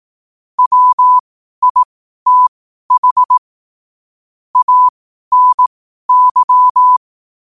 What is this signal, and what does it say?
Morse code, heard as a keyed tone, about 18 wpm, spaced as for 12 wpm, 1000 Hz. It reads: WITH ANY
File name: qotdfiles / 12wpm